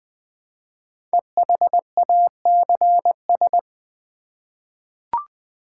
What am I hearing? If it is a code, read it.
Morse code, heard as a keyed tone, about 20 wpm, 700 Hz. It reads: EHACS
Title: Code Groups - 5 Letters - 20wpm